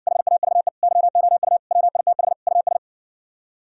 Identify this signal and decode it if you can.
Morse code, heard as a keyed tone, about 69 wpm, 700 Hz. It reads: HAVE YOU WITH US